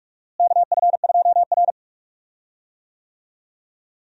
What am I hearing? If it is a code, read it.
Morse code, heard as a keyed tone, about 42 wpm, 700 Hz. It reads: KF2R
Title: USA Call Signs - 4 Characters - Repeated 2x - 45wpm